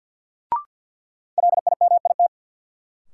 Morse code, heard as a keyed tone, about 47 wpm, 700 Hz. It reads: LIMIT
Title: Sentences from Top 1500 Words - New Word Drill Alt - 50wpm